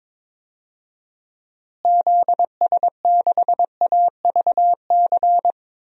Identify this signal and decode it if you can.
Morse code, heard as a keyed tone, about 22 wpm, 700 Hz. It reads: ZS6AVC
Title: DXCC-100 Call Signs - 6 Characters - Repeated 3x - 22wpm